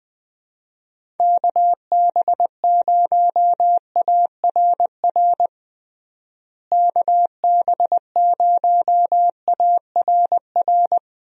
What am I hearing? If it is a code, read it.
Morse code, heard as a keyed tone, about 20 wpm, 700 Hz. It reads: KB0ARR KB0ARR
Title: USA Call Signs - 6 Characters - Repeated 3x - 20wpm